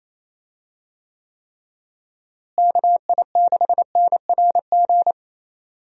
Morse code, heard as a keyed tone, about 28 wpm, 700 Hz. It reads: KI6NRG
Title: USA Call Signs - 6 Characters - Repeated 3x - 28wpm